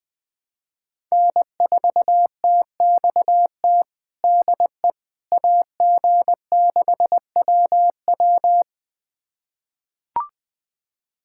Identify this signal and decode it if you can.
Morse code, heard as a keyed tone, about 21 wpm, 700 Hz. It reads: N4TXT DE AG6WW
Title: USA Callsign DE Callsign - 5 Characters - 20wpm